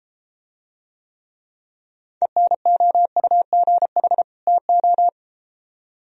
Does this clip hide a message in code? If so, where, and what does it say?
Yes: Morse code from 2.2 to 5.1 s, heard as a keyed tone, about 31 wpm, 700 Hz. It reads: ENOUGH TO